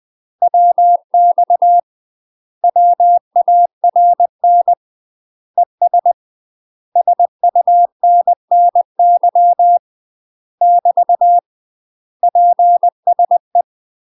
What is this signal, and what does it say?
Morse code, heard as a keyed tone, about 20 wpm, 700 Hz. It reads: WX WARN ES SUNNY <BT> PSE
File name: UK MASTER QSOs TRAINING / 20WPM / UK MASTER QSOs 0063